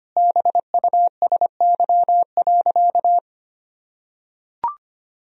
Morse code, heard as a keyed tone, about 25 wpm, 700 Hz. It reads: BUSY.
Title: Sentences from Top 1000 Words - Review Part 3 - 25wpm